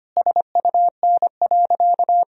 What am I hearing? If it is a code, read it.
Morse code, heard as a keyed tone, about 25 wpm, 700 Hz. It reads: SUN.
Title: Sentences from Top 1000 Words - Review Part 1 - 25wpm